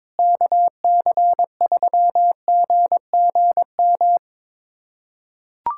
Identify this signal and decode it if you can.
Morse code, heard as a keyed tone, about 22 wpm, 700 Hz. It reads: KC3GGM
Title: USA Callsign DE Callsign - 6 Characters - 22wpm